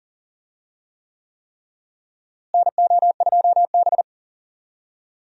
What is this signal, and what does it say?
Morse code, heard as a keyed tone, about 39 wpm, 700 Hz. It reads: NO2B
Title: USA Call Signs - 4 Characters - Repeated 3x - 40wpm